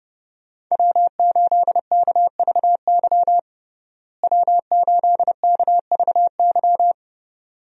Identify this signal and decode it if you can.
Morse code, heard as a keyed tone, about 30 wpm, 700 Hz. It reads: W8KVY W8KVY
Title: USA Call Signs - 5 Characters - Repeated 2x - 30wpm